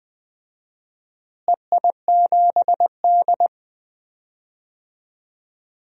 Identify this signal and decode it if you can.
Morse code, heard as a keyed tone, about 20 wpm, 700 Hz. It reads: EI7D
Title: DXCC-100 Call Signs - 4 Characters - Repeated 2x - 20wpm